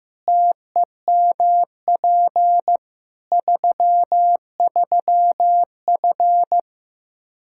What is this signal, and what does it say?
Morse code, heard as a keyed tone, about 15 wpm, 700 Hz. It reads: TEMP 33F
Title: Practice QSO - Exchange 2 - Info - 15wpm